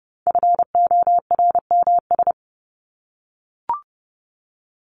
Morse code, heard as a keyed tone, about 30 wpm, 700 Hz. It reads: FORMS
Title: ES:S Suffix - Top 100 Words with Base Words 30wpm